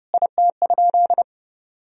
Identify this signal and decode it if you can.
Morse code, heard as a keyed tone, about 32 wpm, 700 Hz. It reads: IT?
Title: Sentences from Top 100 Words and Abbreviations 30wpm